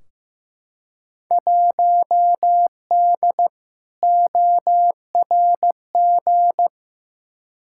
Morse code, heard as a keyed tone, about 15 wpm, 700 Hz. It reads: 1D ORG
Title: Field Day Exchanges - 15wpm